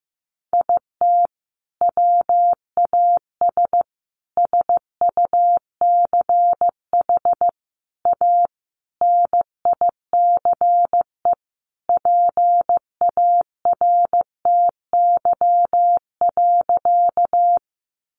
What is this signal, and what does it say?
Morse code, heard as a keyed tone, about 15 wpm, 700 Hz. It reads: IT WAS SUCH A NICE PARTY.
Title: Sentences from Top 1400 Words 15wpm